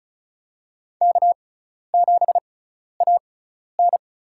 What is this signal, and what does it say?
Morse code, heard as a keyed tone, about 34 wpm, spaced as for 12 wpm, 700 Hz. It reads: K7AN
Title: USA Call Signs - 4 Characters - 6x Character Spacing - 35wpm